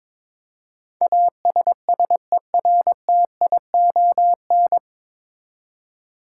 Morse code, heard as a keyed tone, about 22 wpm, 700 Hz. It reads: ASSERTION
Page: Morse Code Ninja practice 22wpm